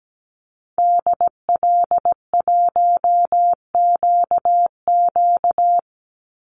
Morse code, heard as a keyed tone, about 17 wpm, 700 Hz. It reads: DL1QQ
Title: Call Signs Repeated 3x - 17wpm